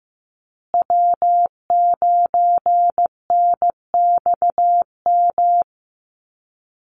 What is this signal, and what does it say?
Morse code, heard as a keyed tone, about 15 wpm, 700 Hz. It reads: W9NXM
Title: Call Signs Repeated 2x - 15wpm